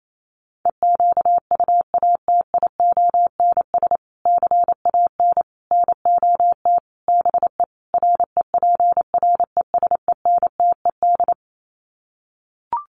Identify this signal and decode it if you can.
Morse code, heard as a keyed tone, about 28 wpm, 700 Hz. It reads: EQUATIONS CAN NOT BE REPRESENTED
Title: Sets of 5 Words 28wpm